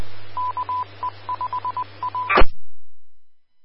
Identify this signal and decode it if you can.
Morse code, heard as a keyed tone, about 20 wpm, 1000 Hz. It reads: KE5A
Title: Austin, TX Amateur Repeaters